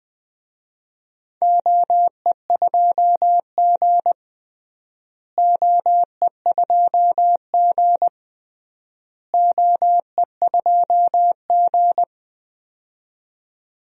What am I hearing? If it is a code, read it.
Morse code, heard as a keyed tone, about 20 wpm, 700 Hz. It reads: OE2G OE2G OE2G
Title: DXCC-100 Call Signs - 4 Characters - Repeated 3x - 20wpm